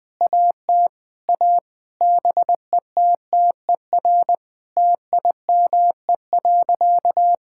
Morse code, heard as a keyed tone, about 20 wpm, 700 Hz. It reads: AT A BETTER TIME.
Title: Sentences with 8-Word Phrases - Spoken Beginning - 20wpm